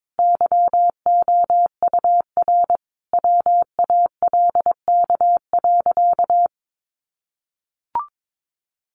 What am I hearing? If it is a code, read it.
Morse code, heard as a keyed tone, about 22 wpm, 700 Hz. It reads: YOUR WALK.
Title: Sentences from Top 800 Words 22wpm